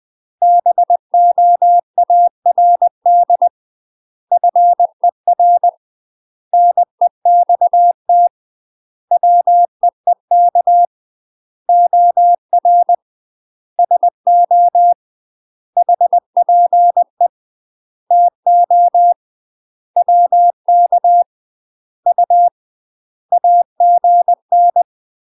Morse code, heard as a keyed tone, about 20 wpm, 700 Hz. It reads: BOARD FER NEXT WEEK OR SO HPE TO WK U AGN
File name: UK MASTER QSOs TRAINING / 20WPM / UK MASTER QSOs 0026